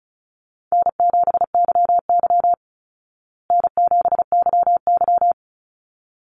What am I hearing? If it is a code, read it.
Morse code, heard as a keyed tone, about 34 wpm, 700 Hz. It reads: N7YY N7YY
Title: Easy Call Signs Repeated 2x - 35wpm